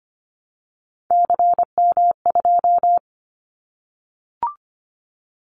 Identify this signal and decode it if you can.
Morse code, heard as a keyed tone, about 25 wpm, 700 Hz. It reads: CM2
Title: Callsign Prefixes for Top 100 DXCC - 25wpm